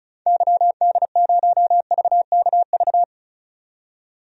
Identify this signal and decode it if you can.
Morse code, heard as a keyed tone, about 36 wpm, 700 Hz. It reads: YD0VKV
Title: DXCC-100 Call Signs - 6 Characters - 35wpm